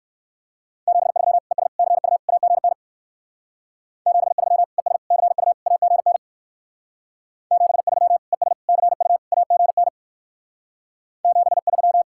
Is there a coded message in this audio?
Yes: Morse code from 0.9 to 12.0 s, heard as a keyed tone, about 62 wpm, 700 Hz. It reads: 73 ES CU AGN 73 ES CU AGN 73 ES CU AGN 73